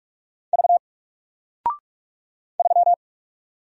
Morse code, heard as a keyed tone, about 44 wpm, 700 Hz. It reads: V 3
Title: Single Character (Letter:Number:Punctuation:Prosign) 45wpm